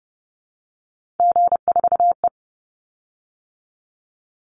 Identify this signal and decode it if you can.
Morse code, heard as a keyed tone, about 30 wpm, 700 Hz. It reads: G4E
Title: Call Signs Repeated 2x - 30wpm